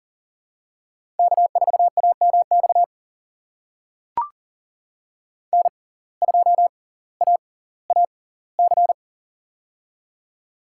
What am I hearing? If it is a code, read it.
Morse code, heard as a keyed tone, about 40 wpm, spaced as for 20 wpm, 700 Hz. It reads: K4AMX N2AAC
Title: USA Call Signs - 5 Characters - 6x Character Spacing - 40wpm